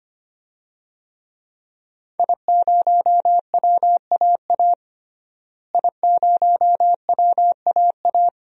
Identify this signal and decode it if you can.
Morse code, heard as a keyed tone, about 25 wpm, 700 Hz. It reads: I0WAA I0WAA
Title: DXCC-100 Call Signs - 5 Characters - Repeated 2x - 25wpm